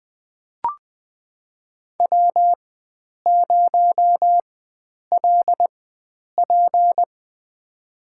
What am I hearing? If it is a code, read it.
Morse code, heard as a keyed tone, about 20 wpm, spaced as for 9 wpm, 700 Hz. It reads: W0LP
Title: USA Call Signs - 4 Characters - 4x Character Spacing - 20wpm